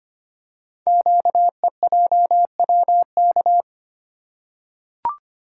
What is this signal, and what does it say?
Morse code, heard as a keyed tone, about 25 wpm, 700 Hz. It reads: QEJWK
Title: Code Groups - 5 Letters - 25wpm